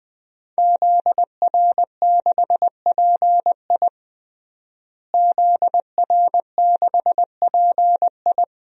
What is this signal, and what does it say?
Morse code, heard as a keyed tone, about 20 wpm, 700 Hz. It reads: ZR6PI ZR6PI
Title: DXCC-100 Call Signs - 5 Characters - Repeated 2x - 20wpm